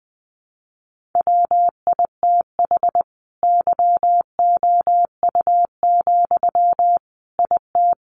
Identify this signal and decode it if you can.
Morse code, heard as a keyed tone, about 20 wpm, 700 Hz. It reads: WITH YOU, IT
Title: Sentences from Top 1400 Words 20wpm